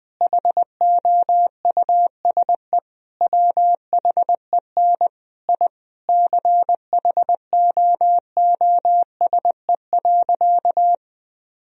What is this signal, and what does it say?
Morse code, heard as a keyed tone, about 20 wpm, 700 Hz. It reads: HOUSE WHEN I CHOOSE.
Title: Sentences from Top 1000 Words - Review Part 4 - 20wpm